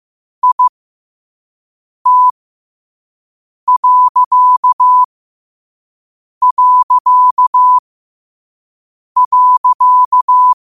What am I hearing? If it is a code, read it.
Morse code, heard as a keyed tone, about 15 wpm, spaced as for 5 wpm, 1000 Hz. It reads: IT...
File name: qotdfiles / 5wpm